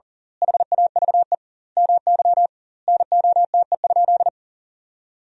Morse code, heard as a keyed tone, about 38 wpm, 700 Hz. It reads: HAVE MY NOTE?
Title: 2x Word-Spacing - Sentences from Top 500 Words - 40wpm